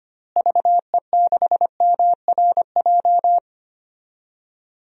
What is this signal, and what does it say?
Morse code, heard as a keyed tone, about 25 wpm, 700 Hz. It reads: VE6MRJ
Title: DXCC-100 Call Signs - 6 Characters - Repeated 2x - 25wpm